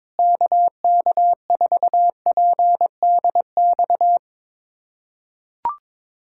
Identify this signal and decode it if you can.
Morse code, heard as a keyed tone, about 22 wpm, 700 Hz. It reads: KK4PDX
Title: USA Callsign DE Callsign - 6 Characters - 22wpm